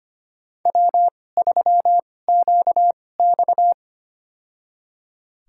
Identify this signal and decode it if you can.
Morse code, heard as a keyed tone, about 25 wpm, spaced as for 18 wpm, 700 Hz. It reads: W3QX
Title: USA Call Signs - 4 Characters - 2x Character Spacing - 25wpm